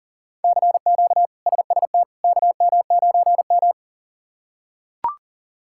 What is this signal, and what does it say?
Morse code, heard as a keyed tone, about 37 wpm, 700 Hz. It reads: CQ SST KM9M
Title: K1USN SST QSO Practice - 40wpm